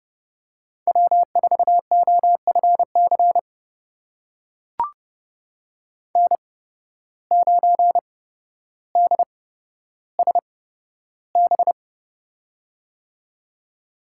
Morse code, heard as a keyed tone, about 29 wpm, spaced as for 13 wpm, 700 Hz. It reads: W4OFC N9DSB
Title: USA Call Signs - 5 Characters - 8x Character Spacing - 30wpm